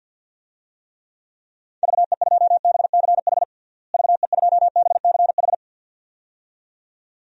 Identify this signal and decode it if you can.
Morse code, heard as a keyed tone, about 48 wpm, 700 Hz. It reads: VE2BCH VE2BCH
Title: DXCC-100 Call Signs - 6 Characters - Repeated 2x - 50wpm